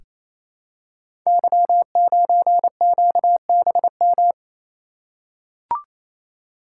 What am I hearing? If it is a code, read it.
Morse code, heard as a keyed tone, about 27 wpm, 700 Hz. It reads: Y9QBM
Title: Code Groups - 5 Letter:Numbers - 28wpm